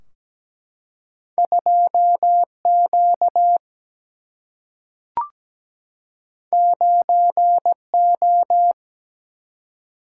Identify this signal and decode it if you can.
Morse code, heard as a keyed tone, about 17 wpm, 700 Hz. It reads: 2 Q 9 O
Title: Single-Letter Single-Number 17wpm